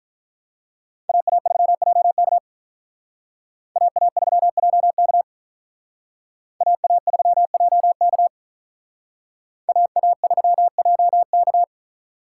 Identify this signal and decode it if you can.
Morse code, heard as a keyed tone, about 43 wpm, 700 Hz. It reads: AA3JK AA3JK AA3JK AA3JK
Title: Speed Racing - USA 2x2 Call Signs 35wpm